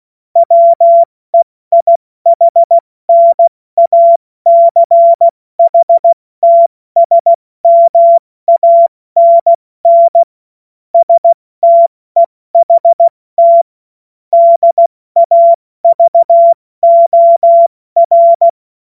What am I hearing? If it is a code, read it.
Morse code, heard as a keyed tone, about 16 wpm, spaced as for 14 wpm, 680 Hz. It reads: WEIHNACHTSMANN STEHT DAVOR
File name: Morsekurs / Witze / Witz 24. Januar 2023 70 F80